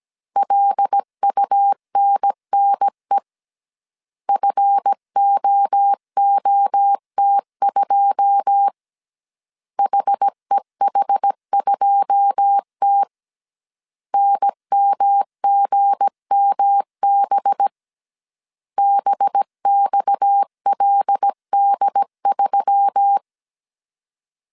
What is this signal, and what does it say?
Morse code, heard as a keyed tone, about 17 wpm, 800 Hz. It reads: LUNNE FOOT2 HEH2T NMGMB BXLD3
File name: CW / DET / 1 9 1500C S A 16wpm 2828